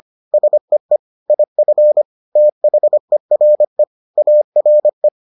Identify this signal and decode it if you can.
Morse code, heard as a keyed tone, about 25 wpm, 600 Hz. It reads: SEE IF THERE ARE